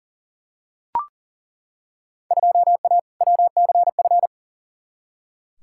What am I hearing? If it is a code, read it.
Morse code, heard as a keyed tone, about 37 wpm, 700 Hz. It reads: 2A WCF
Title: Field Day Exchanges Encore - 40wpm